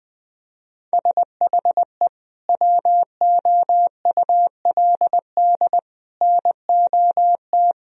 Morse code, heard as a keyed tone, about 20 wpm, 700 Hz. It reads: SHE WOULD NOT
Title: Sentences from Top 1000 Words - Review Part 3 - 20wpm